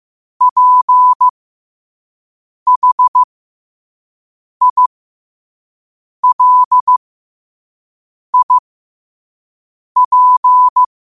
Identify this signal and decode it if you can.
Morse code, heard as a keyed tone, about 15 wpm, spaced as for 5 wpm, 1000 Hz. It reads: PHILIP
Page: Quotes for Wed, 13 Aug 2025 in Morse Code at 5 words per minute.